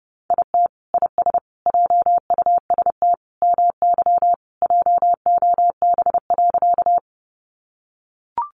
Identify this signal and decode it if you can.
Morse code, heard as a keyed tone, about 30 wpm, 700 Hz. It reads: IT IS JUST MY JOB.
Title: Sentences from Top 600 Words 30wpm